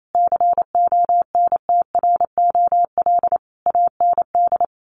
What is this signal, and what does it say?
Morse code, heard as a keyed tone, about 28 wpm, 700 Hz. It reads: CONTROL AND
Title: Sets of 5 Words Encore2 28wpm